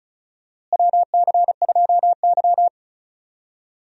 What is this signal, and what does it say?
Morse code, heard as a keyed tone, about 34 wpm, 700 Hz. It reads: WC2Y
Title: USA Call Signs - 4 Characters - Repeated 2x - 35wpm